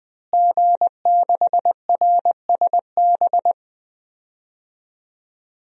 Morse code, heard as a keyed tone, about 20 wpm, 700 Hz. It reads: G6RSB
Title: DXCC-100 Call Signs - 5 Characters - Repeated 3x - 20wpm